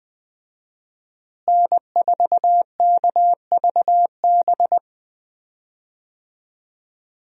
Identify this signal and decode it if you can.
Morse code, heard as a keyed tone, about 21 wpm, 700 Hz. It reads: N4KVB
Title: USA Call Signs - 5 Characters - Repeated 3x - 20wpm